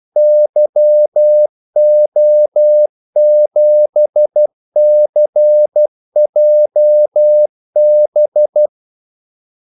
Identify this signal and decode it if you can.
Morse code, heard as a keyed tone, about 12 wpm, 600 Hz. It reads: YO7CJB
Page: 2 ekstra ordmellemrum mellem kaldesignalerne